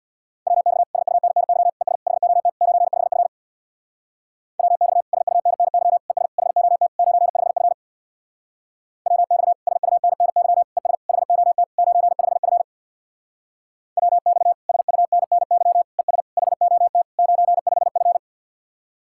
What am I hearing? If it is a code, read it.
Morse code, heard as a keyed tone, about 59 wpm, 700 Hz. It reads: WX SUNNY ES HOT 95F WX SUNNY ES HOT 95F WX SUNNY ES HOT 95F WX SUNNY ES HOT 95F
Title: Speed-Racing Common QSO Phrases Encore - 50wpm